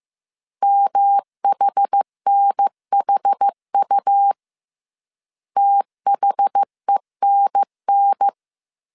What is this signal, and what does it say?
Morse code, heard as a keyed tone, about 15 wpm, 800 Hz. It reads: MHNHU THENN